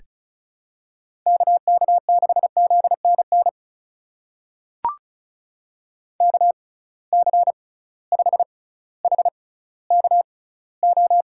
Morse code, heard as a keyed tone, about 34 wpm, spaced as for 18 wpm, 700 Hz. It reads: KK6ZNN KC5HKO
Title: USA Call Signs - 6 Characters - 6x Character Spacing - 35wpm